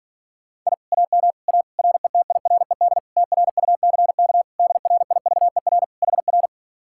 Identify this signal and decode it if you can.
Morse code, heard as a keyed tone, about 48 wpm, 700 Hz. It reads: I AM A RETIRED TRUCK DRIVER HR